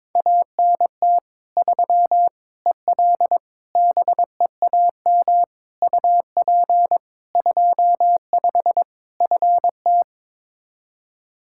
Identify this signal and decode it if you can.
Morse code, heard as a keyed tone, about 22 wpm, 700 Hz. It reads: ANT 3 EL BEAM UP 25 FT
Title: K1USN SST QSO Practice 22wpm